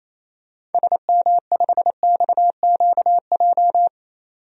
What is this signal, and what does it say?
Morse code, heard as a keyed tone, about 28 wpm, 700 Hz. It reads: SM5XQJ
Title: DXCC-100 Call Signs - 6 Characters - 28wpm